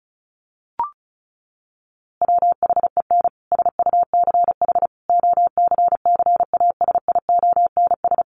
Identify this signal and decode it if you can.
Morse code, heard as a keyed tone, about 34 wpm, 700 Hz. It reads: WHEN SUCH OCCASIONS
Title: A Sound Of Thunder 35wpm